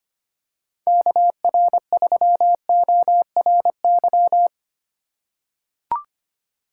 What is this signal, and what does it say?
Morse code, heard as a keyed tone, about 25 wpm, 700 Hz. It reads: KR3ORY
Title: USA Call Signs - 6 Characters - 4x Character Spacing - 25wpm